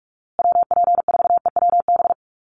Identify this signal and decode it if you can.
Morse code, heard as a keyed tone, about 44 wpm, 700 Hz. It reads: WP4EWB
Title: USA Call Signs Repeated 2x 45wpm